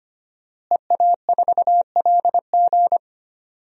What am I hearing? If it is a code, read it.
Morse code, heard as a keyed tone, about 25 wpm, 700 Hz. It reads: EA4LG
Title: DXCC-100 Call Signs - 5 Characters - Repeated 3x - 25wpm